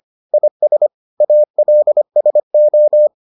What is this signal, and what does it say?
Morse code, heard as a keyed tone, about 25 wpm, 600 Hz. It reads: IS ALSO